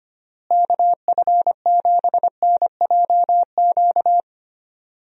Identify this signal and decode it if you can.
Morse code, heard as a keyed tone, about 25 wpm, 700 Hz. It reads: KF7NJQ
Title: USA Call Signs - 6 Characters - Repeated 3x - 25wpm